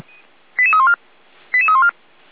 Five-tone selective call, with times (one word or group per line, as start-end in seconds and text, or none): 0.6-1.0 88314
1.5-1.9 88314